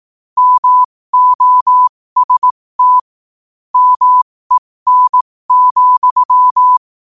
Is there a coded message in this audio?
Yes: Morse code from 0.4 to 6.8 s, heard as a keyed tone, about 18 wpm, spaced as for 15 wpm, 1000 Hz. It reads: MOST MEN,